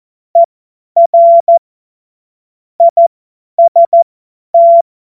Morse code, heard as a keyed tone, about 14 wpm, spaced as for 10 wpm, 680 Hz. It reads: ER IST
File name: Morsekurs / Witze / Witz 26. Oktober 2021, 50 F70